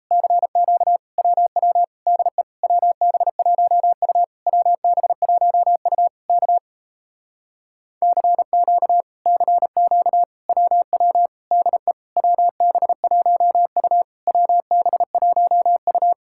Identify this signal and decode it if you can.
Morse code, heard as a keyed tone, about 38 wpm, 700 Hz. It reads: CQ WW DE WB1U WB1U K CQ CQ WW DE WB1U WB1U